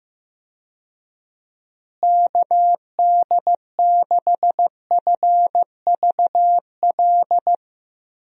Morse code, heard as a keyed tone, about 15 wpm, 700 Hz. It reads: KD6FVL
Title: USA Call Signs - 6 Characters - Repeated 2x - 15wpm